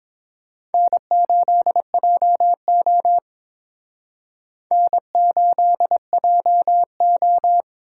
Morse code, heard as a keyed tone, about 28 wpm, 700 Hz. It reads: N8JO N8JO